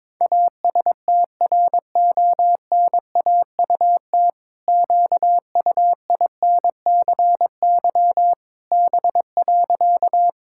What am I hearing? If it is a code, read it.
Morse code, heard as a keyed tone, about 22 wpm, 700 Hz. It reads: ASTRONAUT QUINCY B.
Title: Pangram Challenge v2 - 22wpm